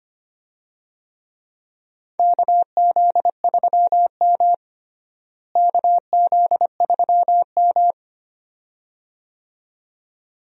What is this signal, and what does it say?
Morse code, heard as a keyed tone, about 25 wpm, 700 Hz. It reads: KZ3M KZ3M
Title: USA Call Signs - 4 Characters - Repeated 2x - 25wpm